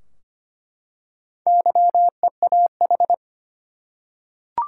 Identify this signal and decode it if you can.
Morse code, heard as a keyed tone, about 25 wpm, 700 Hz. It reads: YEAH
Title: Four Letter Words - 25wpm